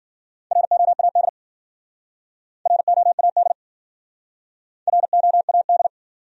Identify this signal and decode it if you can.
Morse code, heard as a keyed tone, about 54 wpm, 700 Hz. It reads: ROAD ROAD ROAD